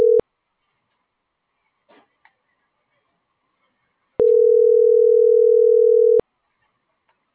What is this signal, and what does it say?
Ringback tone (North America), 440 + 480 Hz.